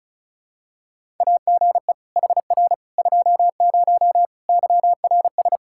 Morse code, heard as a keyed tone, about 36 wpm, 700 Hz. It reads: AGE HR 20 YRS
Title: Practice QSO - Exchange 3 - More Info - 35wpm